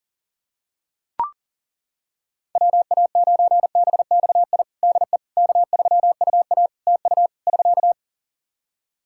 Morse code, heard as a keyed tone, about 37 wpm, 700 Hz. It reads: WA9BXI DE K3UA TU <SK>
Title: K1USN SST QSO Practice 40wpm